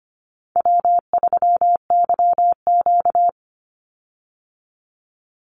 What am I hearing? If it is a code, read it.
Morse code, heard as a keyed tone, about 25 wpm, 700 Hz. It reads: W3YQ
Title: Easy Call Signs Repeated 2x - 25wpm